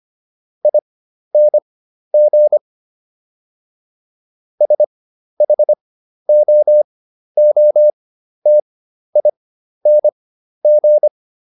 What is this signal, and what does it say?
Morse code, heard as a keyed tone, about 25 wpm, spaced as for 12 wpm, 600 Hz. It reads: ING SHOOTING